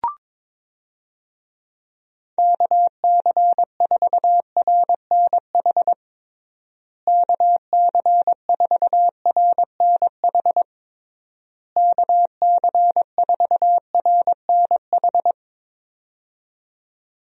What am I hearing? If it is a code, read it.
Morse code, heard as a keyed tone, about 22 wpm, 700 Hz. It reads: KC4RNH KC4RNH KC4RNH
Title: USA Call Signs - 6 Characters - Repeated 3x - 22wpm